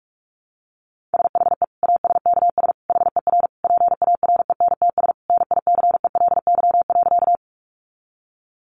Morse code, heard as a keyed tone, about 44 wpm, 700 Hz. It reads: SHE ASKS HER PARENTS NICELY.
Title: Easy Phrases 45wpm